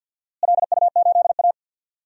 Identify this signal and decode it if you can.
Morse code, heard as a keyed tone, about 50 wpm, 700 Hz. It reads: LU8A
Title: DXCC-100 Call Signs - 4 Characters - Repeated 3x - 50wpm